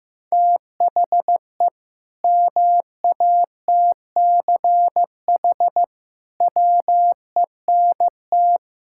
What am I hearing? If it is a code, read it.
Morse code, heard as a keyed tone, about 15 wpm, 700 Hz. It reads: THE MATCH WENT
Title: Sentences from Top 1600 Words 15wpm